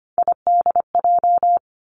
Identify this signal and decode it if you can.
Morse code, heard as a keyed tone, about 25 wpm, 700 Hz. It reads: IDJ
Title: Three Challenging Letter-Number Combinations 25wpm